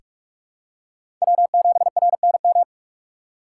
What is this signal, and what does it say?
Morse code, heard as a keyed tone, about 46 wpm, 700 Hz. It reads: W7RNM
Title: USA Call Signs - 5 Characters - 4x Character Spacing - 45wpm